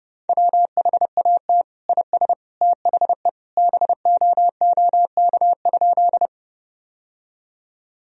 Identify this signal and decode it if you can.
Morse code, heard as a keyed tone, about 32 wpm, 700 Hz. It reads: WHAT IS THE BOOK?
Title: Sentences from Top 1000 Words - Review Part 1 - 30wpm